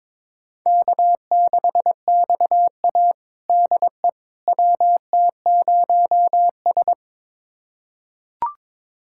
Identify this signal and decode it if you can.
Morse code, heard as a keyed tone, about 21 wpm, 700 Hz. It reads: K6XA DE WT0S
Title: USA Callsign DE Callsign - 4 Characters - 22wpm